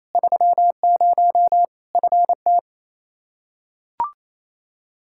Morse code, heard as a keyed tone, about 28 wpm, 700 Hz. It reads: 30 FT
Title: Practice QSO - Exchange 2 - Info - 28wpm